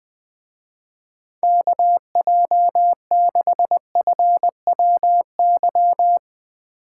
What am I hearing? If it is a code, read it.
Morse code, heard as a keyed tone, about 20 wpm, 700 Hz. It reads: KJ6FWY